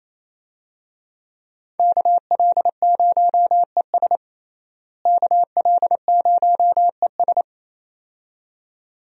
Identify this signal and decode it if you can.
Morse code, heard as a keyed tone, about 28 wpm, 700 Hz. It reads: KL0ES KL0ES
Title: USA Call Signs - 5 Characters - Repeated 2x - 28wpm